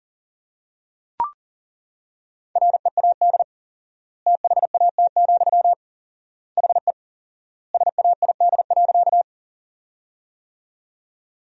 Morse code, heard as a keyed tone, about 40 wpm, 700 Hz. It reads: READ THAT, HE SAID.
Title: 4x Word-Spacing - Sentences from Top 200 Words - 40wpm